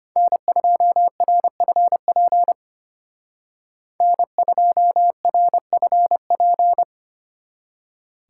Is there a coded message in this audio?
Yes: Morse code from 0.2 to 6.8 s, heard as a keyed tone, about 32 wpm, 700 Hz. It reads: N2RFP N2RFP